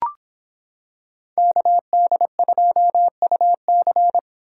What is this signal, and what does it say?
Morse code, heard as a keyed tone, about 26 wpm, 700 Hz. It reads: KD2UC